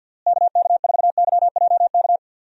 Morse code, heard as a keyed tone, about 50 wpm, 700 Hz. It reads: KK4YJK
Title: USA Call Signs - 6 Characters - Repeated 2x - 50wpm